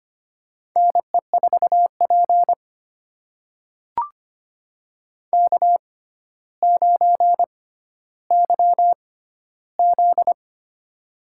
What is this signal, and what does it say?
Morse code, heard as a keyed tone, about 25 wpm, spaced as for 13 wpm, 700 Hz. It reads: NE4P K9YZ
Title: USA Call Signs - 4 Characters - 6x Character Spacing - 25wpm